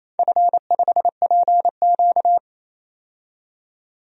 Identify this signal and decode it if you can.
Morse code, heard as a keyed tone, about 27 wpm, 700 Hz. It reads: F5PQ
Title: DXCC-100 Call Signs - 4 Characters - 28wpm